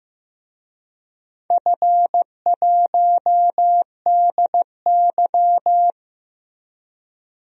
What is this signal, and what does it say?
Morse code, heard as a keyed tone, about 15 wpm, 700 Hz. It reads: F1DY
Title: DXCC-100 Call Signs - 4 Characters - 15wpm